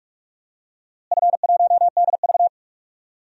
Morse code, heard as a keyed tone, about 43 wpm, 700 Hz. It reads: F1DV
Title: DXCC-100 Call Signs - 4 Characters - Repeated 2x - 45wpm